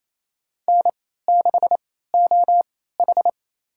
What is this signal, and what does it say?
Morse code, heard as a keyed tone, about 28 wpm, spaced as for 16 wpm, 700 Hz. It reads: N6OH
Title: USA Call Signs - 4 Characters - 3x Character Spacing - 28wpm